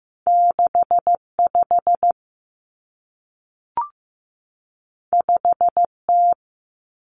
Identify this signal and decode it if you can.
Morse code, heard as a keyed tone, about 15 wpm, 700 Hz. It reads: 65 5T
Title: Common Cut Numbers - 2-Digit 15wpm